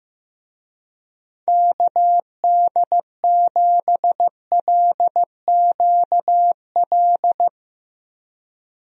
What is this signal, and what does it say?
Morse code, heard as a keyed tone, about 16 wpm, 700 Hz. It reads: KD7LQL
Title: USA Call Signs - 6 Characters - Repeated 2x - 15wpm